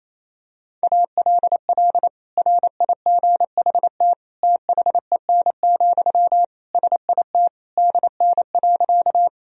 Morse code, heard as a keyed tone, about 28 wpm, 700 Hz. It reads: ALL RIGHT THEN, SIT DN.
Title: Sentences from Top 500 Words and Abbreviations 28wpm